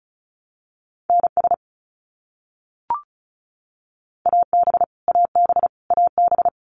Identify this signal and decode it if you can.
Morse code, heard as a keyed tone, about 34 wpm, 700 Hz. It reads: NS AB AB AB
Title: Canadian Provinces and Territories - Repeated 3x - 35wpm